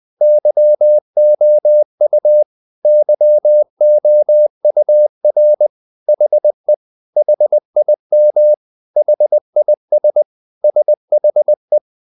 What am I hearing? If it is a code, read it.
Morse code, heard as a keyed tone, about 20 wpm, 600 Hz. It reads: YOU YOUR HE HIM HIS SHE
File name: CW / Practice / 500MostCommonEnglishWords20